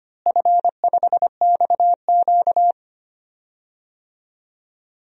Morse code, heard as a keyed tone, about 25 wpm, 700 Hz. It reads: F5XQ
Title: DXCC-100 Call Signs - 4 Characters - Repeated 3x - 25wpm